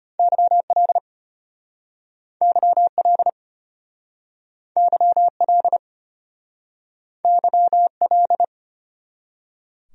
Morse code, heard as a keyed tone, about 36 wpm, 700 Hz. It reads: YL YL YL YL